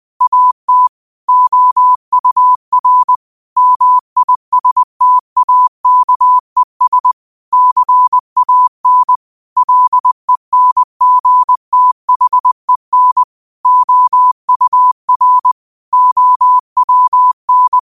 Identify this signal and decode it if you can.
Morse code, heard as a keyed tone, about 20 wpm, 1000 Hz. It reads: AT OUR MISTAKES CAN LENGTHEN OUR OWN